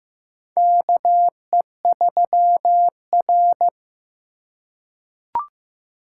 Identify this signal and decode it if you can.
Morse code, heard as a keyed tone, about 15 wpm, 700 Hz. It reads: KE3R
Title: USA Call Signs - 4 Characters - 3x Character Spacing - 15wpm